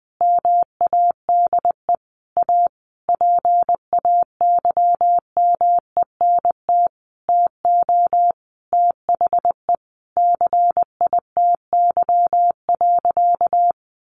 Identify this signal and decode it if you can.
Morse code, heard as a keyed tone, about 19 wpm, 700 Hz. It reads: MADE A PAYMENT TO THE CITY.
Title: Sentences from Top 1400 Words 20wpm